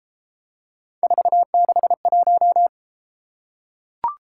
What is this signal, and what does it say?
Morse code, heard as a keyed tone, about 33 wpm, 700 Hz. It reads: 461